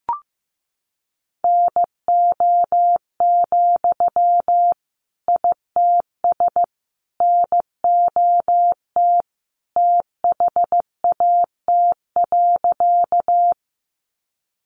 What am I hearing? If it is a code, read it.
Morse code, heard as a keyed tone, about 14 wpm, 700 Hz. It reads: NO, ITS NOT THAT.
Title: Sentences from Top 600 Words 15wpm